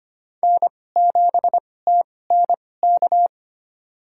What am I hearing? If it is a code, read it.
Morse code, heard as a keyed tone, about 25 wpm, spaced as for 18 wpm, 700 Hz. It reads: N7TNK
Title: USA Call Signs - 2x Character Spacing - 25wpm